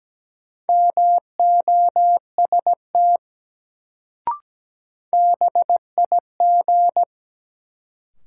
Morse code, heard as a keyed tone, about 17 wpm, 700 Hz. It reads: MOST BIG
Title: Sentences from Top 200 Words - New Word Drill - 17wpm